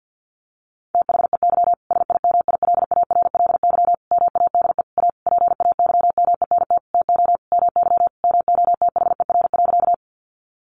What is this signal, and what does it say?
Morse code, heard as a keyed tone, about 50 wpm, 700 Hz. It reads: THEY SIMILARLY MADE A PAYMENT TO MY MOTHER.
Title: Sentences from Top 1400 Words 50wpm